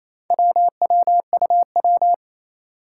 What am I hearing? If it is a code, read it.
Morse code, heard as a keyed tone, about 28 wpm, 700 Hz. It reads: WWUW
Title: Four Challenging Letter-Number Combinations 28wpm